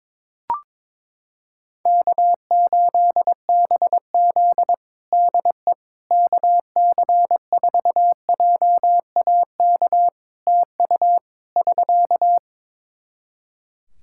Morse code, heard as a keyed tone, about 22 wpm, 700 Hz. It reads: K8BZ DE KC4JAK TU <SK>
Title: Practice QSO - Exchange 4 - Ending - 22wpm